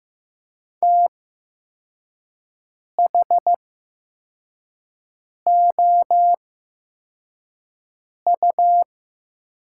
Morse code, heard as a keyed tone, about 15 wpm, spaced as for 4 wpm, 700 Hz. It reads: THOU
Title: 8x Character Spacing - Top 300 Words - 15wpm